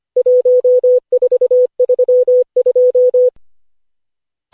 Morse code, heard as a keyed tone, about 25 wpm, 500 Hz. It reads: 1432
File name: cw / Session5 cw training